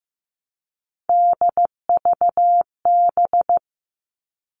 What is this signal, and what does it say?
Morse code, heard as a keyed tone, about 15 wpm, 700 Hz. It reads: DVB
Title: All Cut Numbers - 3-Digit 15wpm